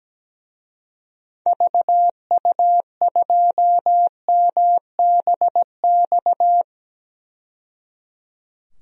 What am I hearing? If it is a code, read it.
Morse code, heard as a keyed tone, about 17 wpm, 700 Hz. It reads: VU2MBX
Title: DXCC-100 Call Signs - 6 Characters - 17wpm